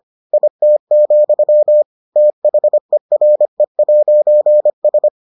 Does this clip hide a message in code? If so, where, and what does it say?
Yes: Morse code from 0.3 to 5.1 s, heard as a keyed tone, about 25 wpm, 600 Hz. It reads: IT, THERE'S